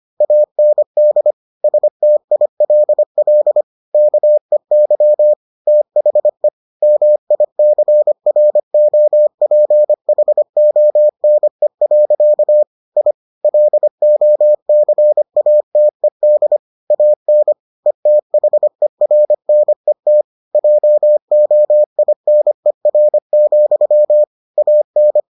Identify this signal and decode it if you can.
Morse code, heard as a keyed tone, about 25 wpm, 600 Hz. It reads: AND STILL KEY THE MICROPHONE. I LOCATED AN ETHERNET JOINER, AN